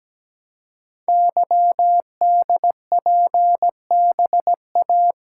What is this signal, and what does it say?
Morse code, heard as a keyed tone, about 17 wpm, 700 Hz. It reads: YDPBA
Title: Code Groups - 5 Letters - 17wpm